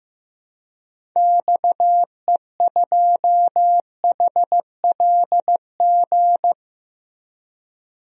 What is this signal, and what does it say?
Morse code, heard as a keyed tone, about 15 wpm, 700 Hz. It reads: XE2HLG
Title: DXCC-100 Call Signs - 6 Characters - Repeated 2x - 15wpm